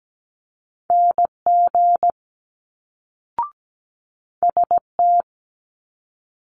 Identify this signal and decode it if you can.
Morse code, heard as a keyed tone, about 17 wpm, 700 Hz. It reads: NG ST
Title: Multi-letter Phonemes Quick Fire 17wpm